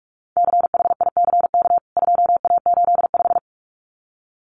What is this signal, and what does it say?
Morse code, heard as a keyed tone, about 44 wpm, 700 Hz. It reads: CHICK 2A85
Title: CWT Exchange 45wpm